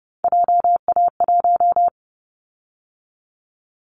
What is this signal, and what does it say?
Morse code, heard as a keyed tone, about 30 wpm, 700 Hz. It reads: JA1
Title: Callsign Prefixes for Top 25 DXCC - 30wpm